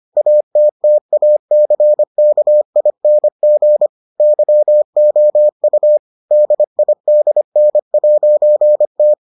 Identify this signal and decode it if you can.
Morse code, heard as a keyed tone, about 25 wpm, 600 Hz. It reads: ATTACKING YOU DIDN'T